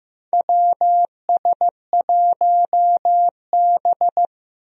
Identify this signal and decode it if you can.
Morse code, heard as a keyed tone, about 15 wpm, 700 Hz. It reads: WS1B
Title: USA Call Signs - 4 Characters - Repeated 2x - 15wpm